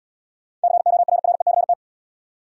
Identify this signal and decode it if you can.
Morse code, heard as a keyed tone, about 76 wpm, 700 Hz. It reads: CORRECT